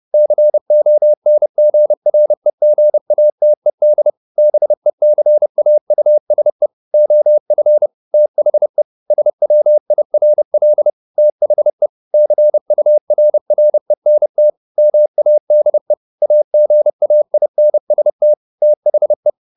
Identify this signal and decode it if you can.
Morse code, heard as a keyed tone, about 30 wpm, 600 Hz. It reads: CONGREGATED BECAUSE OF THE SWIRL THE CURRENT MADE AGAINST THE
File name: the old man and the sea / the old man and the sea 1 30wpm